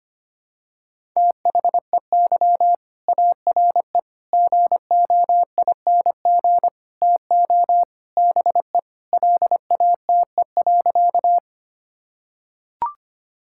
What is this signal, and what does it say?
Morse code, heard as a keyed tone, about 25 wpm, 700 Hz. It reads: THEY ARE GOING TO BE LATE.
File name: practice / 25wpm / Sentences with 4-Word Phrases - 25wpm